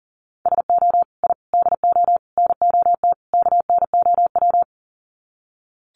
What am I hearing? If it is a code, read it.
Morse code, heard as a keyed tone, about 40 wpm, 700 Hz. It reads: SO I DO NOT KNOW
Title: Sets of 5 Words Encore 40wpm